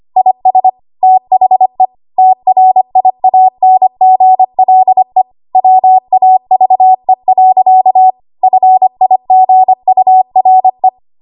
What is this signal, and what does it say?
Morse code, heard as a keyed tone, about 25 wpm, 750 Hz. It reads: IS THE TRIANGLE WAVE. FIGURE
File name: Archive / 25 WPM / 25WPM